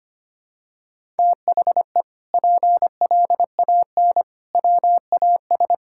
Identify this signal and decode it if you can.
Morse code, heard as a keyed tone, about 25 wpm, 700 Hz. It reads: THE PLAN WAS
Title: Sentences from Top 1000 Words - Review Part 6 - 25wpm